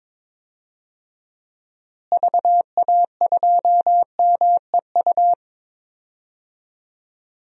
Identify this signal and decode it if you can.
Morse code, heard as a keyed tone, about 22 wpm, 700 Hz. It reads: VA2MEU
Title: DXCC-100 Call Signs - 6 Characters - Repeated 3x - 22wpm